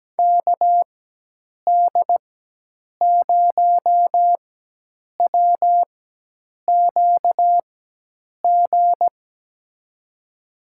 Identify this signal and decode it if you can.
Morse code, heard as a keyed tone, about 17 wpm, spaced as for 8 wpm, 700 Hz. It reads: KD0WQG
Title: USA Call Signs - 6 Characters - 4x Character Spacing - 17wpm